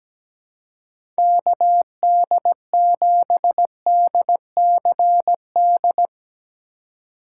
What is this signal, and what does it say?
Morse code, heard as a keyed tone, about 17 wpm, 700 Hz. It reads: KD7DCD
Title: USA Call Signs - 6 Characters - 4x Character Spacing - 17wpm